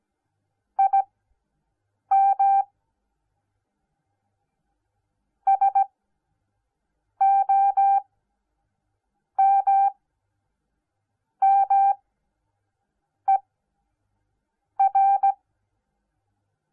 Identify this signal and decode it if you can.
Morse code, heard as a keyed tone, about 17 wpm, spaced as for 5 wpm, 780 Hz. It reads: IM SOMMER